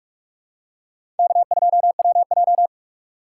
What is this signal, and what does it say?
Morse code, heard as a keyed tone, about 43 wpm, 700 Hz. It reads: K2WJ
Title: USA Call Signs - 4 Characters - 6x Character Spacing - 45wpm